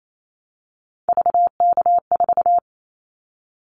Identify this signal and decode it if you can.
Morse code, heard as a keyed tone, about 28 wpm, 700 Hz. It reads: VK4
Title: Callsign Prefixes for Top 100 DXCC - 28wpm